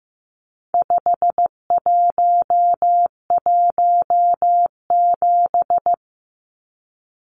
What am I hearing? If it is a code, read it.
Morse code, heard as a keyed tone, about 15 wpm, 700 Hz. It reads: 5117
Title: Numbers - 4-Digit 15wpm